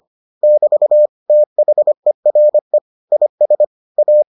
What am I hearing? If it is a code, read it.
Morse code, heard as a keyed tone, about 25 wpm, 600 Hz. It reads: <BT> THERE IS A